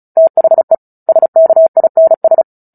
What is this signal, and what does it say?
Morse code, heard as a keyed tone, about 34 wpm, 650 Hz. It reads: THE SKINS
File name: Book Two / 09-Chapter 9 - 35 WPM